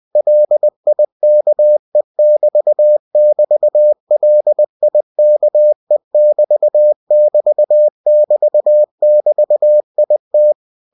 Morse code, heard as a keyed tone, about 20 wpm, 600 Hz. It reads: LIKE<BT><BT>LIKE<BT><BT><BT><BT>IT
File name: cw-books / winnie-the-pooh / winnie-20wpm-page-0043